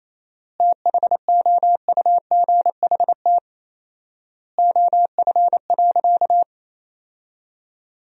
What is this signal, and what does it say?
Morse code, heard as a keyed tone, about 28 wpm, 700 Hz. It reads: THOUGHT OF.
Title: 4x Word-Spacing - Sentences from Top 300 Words - 28wpm